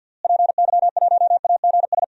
Morse code, heard as a keyed tone, about 50 wpm, 700 Hz. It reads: PY1AGS
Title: DXCC-100 Call Signs - 6 Characters - Repeated 2x - 50wpm